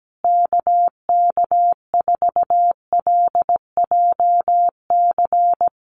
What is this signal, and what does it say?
Morse code, heard as a keyed tone, about 17 wpm, 700 Hz. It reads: KK4LJC
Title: USA Call Signs Repeated 3x 17wpm